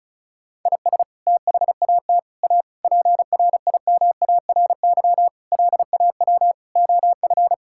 Morse code, heard as a keyed tone, about 36 wpm, 700 Hz. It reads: IS THAT A PRIMARY LAW OF
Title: Sentences from Top 1000 Words - Review Part 5 - 35wpm